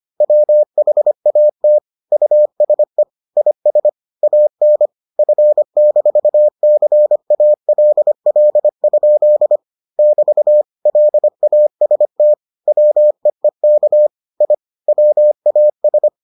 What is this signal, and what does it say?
Morse code, heard as a keyed tone, about 25 wpm, 600 Hz. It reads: WHAT USE IS AN F-CALL? <BT> LAST WEEK I WAS